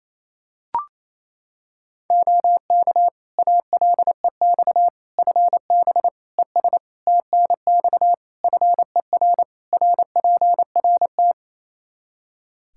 Morse code, heard as a keyed tone, about 28 wpm, 700 Hz. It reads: OK ALEX FB ES TNX FER RPRT
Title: Practice QSO - Exchange 2 - Info - 28wpm